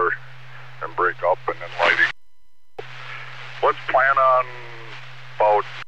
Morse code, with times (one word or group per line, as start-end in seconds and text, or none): none